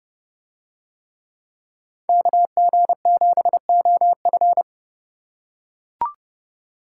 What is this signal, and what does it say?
Morse code, heard as a keyed tone, about 29 wpm, 700 Hz. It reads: KG7OF